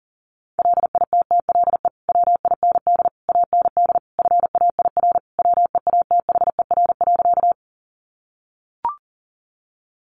Morse code, heard as a keyed tone, about 40 wpm, 700 Hz. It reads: LITTLE WIND AND FAIR WEATHER.
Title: Sentences from Top 1300 Words 40wpm